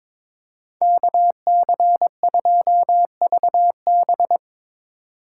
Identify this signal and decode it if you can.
Morse code, heard as a keyed tone, about 24 wpm, 700 Hz. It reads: KC2VB
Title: Speed Racing - USA 2x2 Call Signs 22wpm